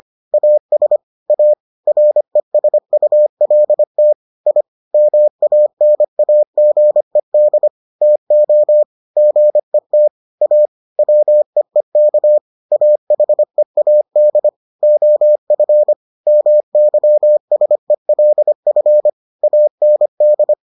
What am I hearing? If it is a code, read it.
Morse code, heard as a keyed tone, about 25 wpm, 600 Hz. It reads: AS A RESULT I MANAGED TO GET A WEEK AHEAD OF MYSELF AND